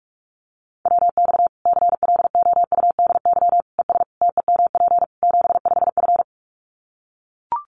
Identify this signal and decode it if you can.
Morse code, heard as a keyed tone, about 44 wpm, 700 Hz. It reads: WX CLOUDY ES TEMP 75F
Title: Common QSO Phrases 45wpm